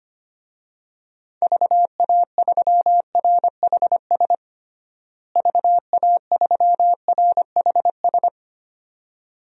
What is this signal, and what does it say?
Morse code, heard as a keyed tone, about 25 wpm, 700 Hz. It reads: VA3RHS VA3RHS
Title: DXCC-100 Call Signs - 6 Characters - Repeated 2x - 25wpm